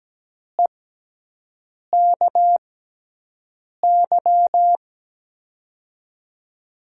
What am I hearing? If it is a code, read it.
Morse code, heard as a keyed tone, about 17 wpm, spaced as for 6 wpm, 700 Hz. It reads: EKY
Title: USA Call Signs - 6 Characters - 6x Character Spacing - 17wpm